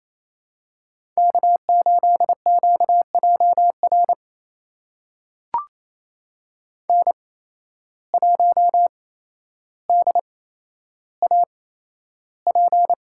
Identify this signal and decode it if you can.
Morse code, heard as a keyed tone, about 28 wpm, spaced as for 12 wpm, 700 Hz. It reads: K8QJR N1DAP
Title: USA Call Signs - 5 Characters - 8x Character Spacing - 28wpm